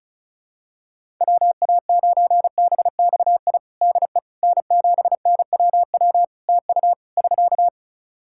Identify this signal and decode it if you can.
Morse code, heard as a keyed tone, about 36 wpm, 700 Hz. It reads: WA9BXI DE N7NWW TU <SK>
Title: Practice QSO - Exchange 4 - Ending - 35wpm